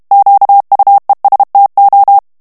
Morse code, heard as a keyed tone, about 32 wpm, 800 Hz. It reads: QUESTO
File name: ITALIEN texte clair 1 vitesse 30 mots minute